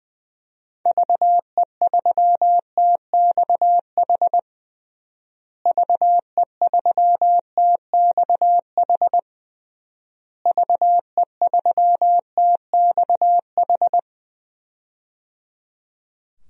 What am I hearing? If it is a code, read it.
Morse code, heard as a keyed tone, about 21 wpm, 700 Hz. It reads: VE3TXH VE3TXH VE3TXH
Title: DXCC-100 Call Signs - 6 Characters - Repeated 3x - 20wpm